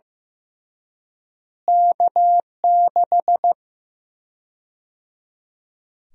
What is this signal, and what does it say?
Morse code, heard as a keyed tone, about 15 wpm, 700 Hz. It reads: K6
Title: USA Call Sign Prefixes - Repeated 2x - 15wpm